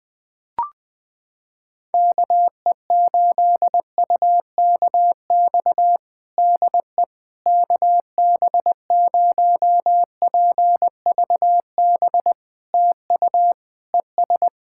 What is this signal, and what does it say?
Morse code, heard as a keyed tone, about 20 wpm, 700 Hz. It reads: KE8UKX DE KB0PVB TU ES
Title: POTA QSO Practice - 20wpm